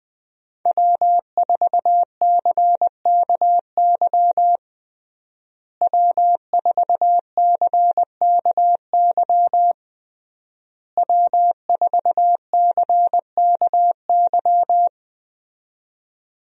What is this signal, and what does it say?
Morse code, heard as a keyed tone, about 20 wpm, 700 Hz. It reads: W4CKY W4CKY W4CKY
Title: USA Call Signs - 5 Characters - Repeated 3x - 20wpm